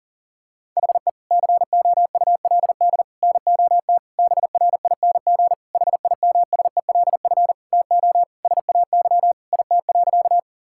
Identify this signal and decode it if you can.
Morse code, heard as a keyed tone, about 40 wpm, 700 Hz. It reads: HE COULD NOT BRING HIMSELF TO SAY IT.
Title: Sentences from Top 1600 Words 40wpm